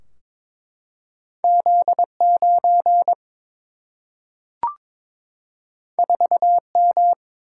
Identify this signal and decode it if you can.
Morse code, heard as a keyed tone, about 22 wpm, 700 Hz. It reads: Z 9 4 M
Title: Single-Letter Single-Number 22wpm